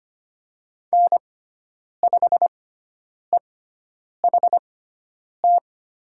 Morse code, heard as a keyed tone, about 25 wpm, spaced as for 9 wpm, 700 Hz. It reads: N5EHT
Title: USA Call Signs - 5 Characters - 6x Character Spacing - 25wpm